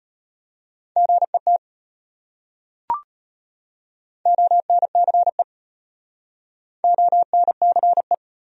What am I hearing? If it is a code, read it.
Morse code, heard as a keyed tone, about 33 wpm, 700 Hz. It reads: GET ONCE ONCE